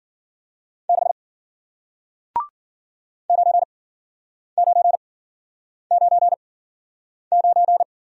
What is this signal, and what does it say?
Morse code, heard as a keyed tone, about 51 wpm, spaced as for 9 wpm, 700 Hz. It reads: 6 9999